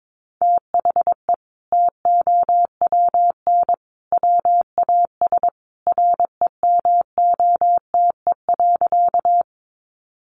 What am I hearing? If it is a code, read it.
Morse code, heard as a keyed tone, about 22 wpm, 700 Hz. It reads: THE TOWN WAS REMOTE.
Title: Sentences from Top 1300 Words 22wpm